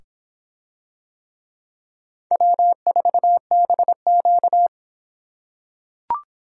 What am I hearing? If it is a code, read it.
Morse code, heard as a keyed tone, about 25 wpm, 700 Hz. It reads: W4BQ